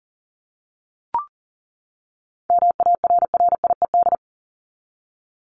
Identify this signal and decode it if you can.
Morse code, heard as a keyed tone, about 40 wpm, 700 Hz. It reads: MARRIED
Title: ED Suffix - Top 100 Words with Base Words 40wpm